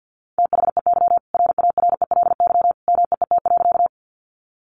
Morse code, heard as a keyed tone, about 50 wpm, 700 Hz. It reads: THEY RARELY MEET.
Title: Sentences from Top 1400 Words 50wpm